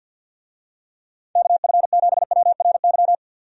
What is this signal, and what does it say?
Morse code, heard as a keyed tone, about 51 wpm, 700 Hz. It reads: KF7WRY
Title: USA Call Signs - 6 Characters - 50wpm